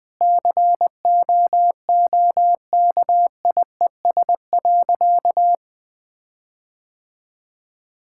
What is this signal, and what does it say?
Morse code, heard as a keyed tone, about 20 wpm, 700 Hz. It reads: COOKIES.
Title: Sentences with 5-Word Phrases - 20wpm